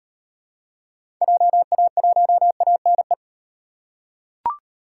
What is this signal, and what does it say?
Morse code, heard as a keyed tone, about 37 wpm, 700 Hz. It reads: JA1ANE